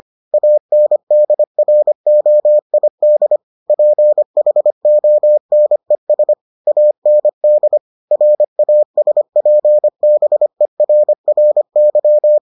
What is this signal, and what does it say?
Morse code, heard as a keyed tone, about 25 wpm, 600 Hz. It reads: ANDROID PHONES AND RASPBERRY